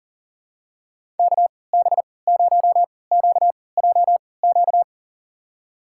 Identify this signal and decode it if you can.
Morse code, heard as a keyed tone, about 40 wpm, spaced as for 23 wpm, 700 Hz. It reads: KB0QJQ
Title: USA Call Signs - 6 Characters - 3x Character Spacing - 40wpm